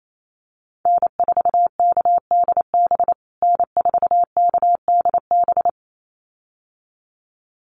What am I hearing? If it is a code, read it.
Morse code, heard as a keyed tone, about 28 wpm, 700 Hz. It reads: N4KDB N4KDB
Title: USA Call Signs Repeated 2x 28wpm